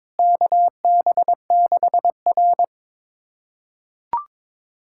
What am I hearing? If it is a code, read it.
Morse code, heard as a keyed tone, about 22 wpm, 700 Hz. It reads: KB6R
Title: USA Callsign DE Callsign - 4 Characters - 22wpm